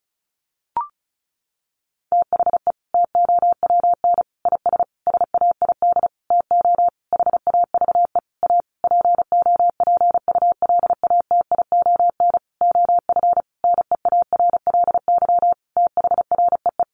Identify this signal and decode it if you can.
Morse code, heard as a keyed tone, about 34 wpm, 700 Hz. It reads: THE TOWN IS SAID TO HAVE A POPULATION OF NEARLY THREE
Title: Sentences from Top 1300 Words 35wpm